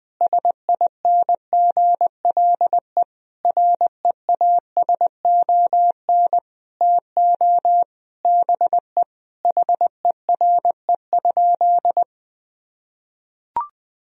Morse code, heard as a keyed tone, about 21 wpm, 700 Hz. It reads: SINGLE REASON TO BE HERE?
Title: Sentences from Top 1000 Words - Review Part 4 - 20wpm